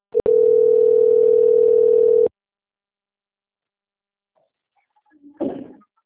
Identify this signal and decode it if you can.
Ringback tone (North America), 440 + 480 Hz.
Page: Limeira 5